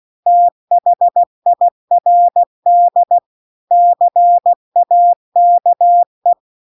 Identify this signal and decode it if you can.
Morse code, heard as a keyed tone, about 16 wpm, 700 Hz. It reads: THIRD CAKE
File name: CW-The-Big-Twoheaded-River-2-16WPM